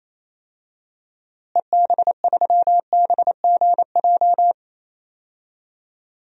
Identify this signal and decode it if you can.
Morse code, heard as a keyed tone, about 28 wpm, 700 Hz. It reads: EB3BGJ
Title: DXCC-100 Call Signs - 6 Characters - Repeated 3x - 28wpm